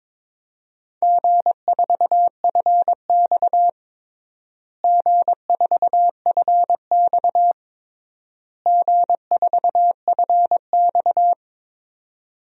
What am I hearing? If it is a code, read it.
Morse code, heard as a keyed tone, about 22 wpm, 700 Hz. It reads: G4FX G4FX G4FX
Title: DXCC-100 Call Signs - 4 Characters - Repeated 3x - 22wpm